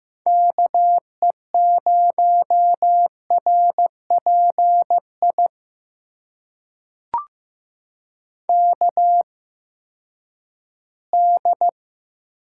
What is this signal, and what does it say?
Morse code, heard as a keyed tone, about 15 wpm, spaced as for 10 wpm, 700 Hz. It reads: KE0RPI KD
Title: USA Call Signs - 8x Character Spacing - 15wpm